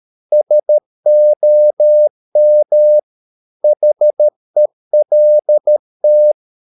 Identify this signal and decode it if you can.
Morse code, heard as a keyed tone, about 13 wpm, 600 Hz. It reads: SOM HELT
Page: Ferie 13 wpm